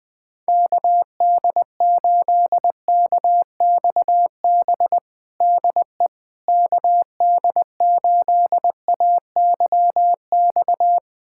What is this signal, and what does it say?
Morse code, heard as a keyed tone, about 20 wpm, 700 Hz. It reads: KD8KXB DE KD8AYX
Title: USA Callsign DE Callsign 20wpm